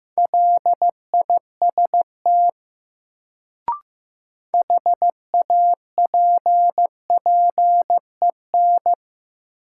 Morse code, heard as a keyed tone, about 15 wpm, 700 Hz. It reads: LIST HAPPEN
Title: Sentences from Top 400 Words - New Word Drill - 15wpm